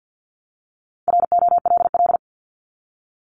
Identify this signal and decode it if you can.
Morse code, heard as a keyed tone, about 50 wpm, 700 Hz. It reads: ROLL
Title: Multi-letter Phonemes Repeated 3x 50wpm